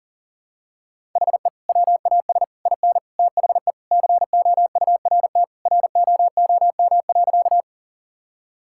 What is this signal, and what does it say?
Morse code, heard as a keyed tone, about 40 wpm, 700 Hz. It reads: HE WAS IN THE COURT ROOM.
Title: Sentences from Top 1600 Words 40wpm